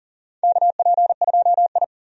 Morse code, heard as a keyed tone, about 40 wpm, 700 Hz. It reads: KP2I
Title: DXCC-100 Call Signs - 4 Characters - 40wpm